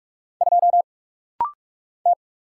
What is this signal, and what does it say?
Morse code, heard as a keyed tone, about 44 wpm, 700 Hz. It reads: 2 T
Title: Single Letter-Number Alt - Rapid-Fire 45wpm